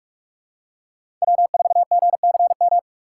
Morse code, heard as a keyed tone, about 46 wpm, 700 Hz. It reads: W4GCM
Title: USA Call Signs - 2x Character Spacing - 45wpm